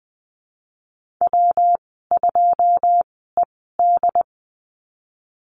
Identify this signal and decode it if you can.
Morse code, heard as a keyed tone, about 20 wpm, spaced as for 15 wpm, 700 Hz. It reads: W2ED
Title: 122 Lesson 20g - Number 3 - Call Signs 20wpm